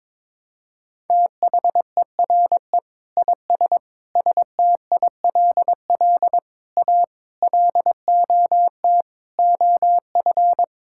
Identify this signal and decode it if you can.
Morse code, heard as a keyed tone, about 22 wpm, 700 Hz. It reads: THERE IS STILL A LOT OF
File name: Sets of 9 Words - 22wpm